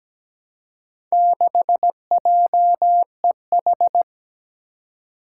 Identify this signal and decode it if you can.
Morse code, heard as a keyed tone, about 17 wpm, 700 Hz. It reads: 6JEH
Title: Four Challenging Letter-Number Combinations 17wpm